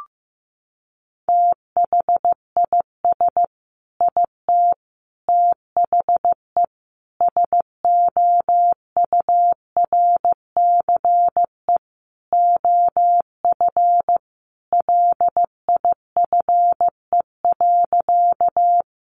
Morse code, heard as a keyed tone, about 15 wpm, 700 Hz. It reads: THIS IT THE SOURCE OF LIFE.
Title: Sentences from Top 800 Words 15wpm